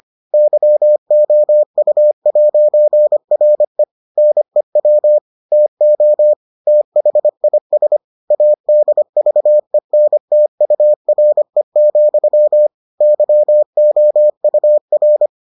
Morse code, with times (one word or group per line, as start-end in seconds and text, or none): 0.3-3.8 YOU'RE
4.2-5.2 NEW
5.5-6.3 TO
6.7-8.0 THIS
8.3-12.7 ADVENTURE,
13.0-15.3 YOUR